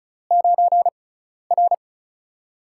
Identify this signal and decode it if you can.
Morse code, heard as a keyed tone, about 36 wpm, spaced as for 12 wpm, 700 Hz. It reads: 9R
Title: USA Call Signs - 4 Characters - 6x Character Spacing - 35wpm